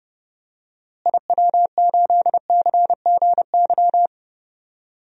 Morse code, heard as a keyed tone, about 32 wpm, 700 Hz. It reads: IW8CGY
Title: DXCC-100 Call Signs - 6 Characters - Repeated 3x - 30wpm